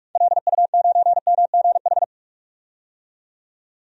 Morse code, heard as a keyed tone, about 46 wpm, 700 Hz. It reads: LU9MGH
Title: DXCC-100 Call Signs - 6 Characters - 45wpm